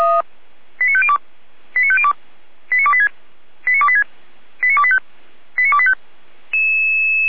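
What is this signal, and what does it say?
Five-tone selective call (ZVEI), heard as quick runs of five tones, tones of about 70 ms. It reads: 89502 89502 89286 89286 89285 89285
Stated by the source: SDR receiver recording of ZVEI five-tone selective call, code 89280.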